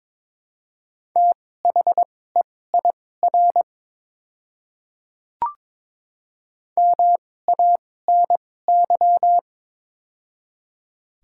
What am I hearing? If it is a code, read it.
Morse code, heard as a keyed tone, about 22 wpm, spaced as for 16 wpm, 700 Hz. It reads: THEIR MANY
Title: 2x Character Spacing - Top 100 Words - 22wpm